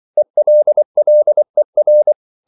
Morse code, heard as a keyed tone, about 24 wpm, 600 Hz. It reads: ELLER
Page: Ferie 23 wpm